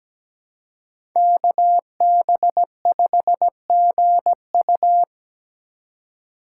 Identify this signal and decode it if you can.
Morse code, heard as a keyed tone, about 17 wpm, 700 Hz. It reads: KB5GU
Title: USA Call Signs - 5 Characters - 4x Character Spacing - 17wpm